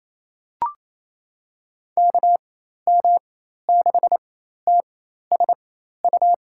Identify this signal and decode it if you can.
Morse code, heard as a keyed tone, about 28 wpm, spaced as for 13 wpm, 700 Hz. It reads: KM6TSU
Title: USA Call Signs - 6 Characters - 4x Character Spacing - 28wpm